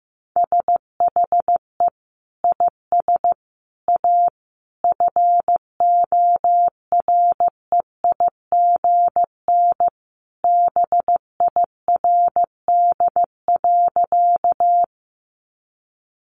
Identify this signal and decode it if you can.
Morse code, heard as a keyed tone, about 15 wpm, 700 Hz. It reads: SHE IS A FOREIGN BIRD.
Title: Sentences from Top 1400 Words 15wpm